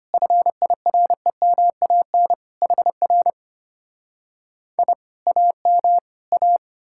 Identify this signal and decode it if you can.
Morse code, heard as a keyed tone, about 33 wpm, 700 Hz. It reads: FIREMAN HR I AM A